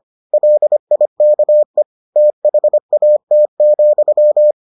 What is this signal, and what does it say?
Morse code, heard as a keyed tone, about 25 wpm, 600 Hz. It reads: LIKE THAT,